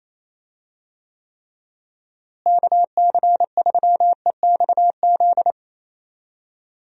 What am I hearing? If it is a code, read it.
Morse code, heard as a keyed tone, about 28 wpm, 700 Hz. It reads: KC3EXZ
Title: USA Call Signs - 6 Characters - Repeated 3x - 28wpm